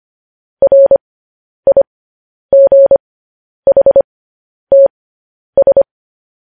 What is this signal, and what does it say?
Morse code, heard as a keyed tone, about 25 wpm, spaced as for 10 wpm, 550 Hz. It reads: RIGHTS
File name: podcast / News headlines-1x-25x10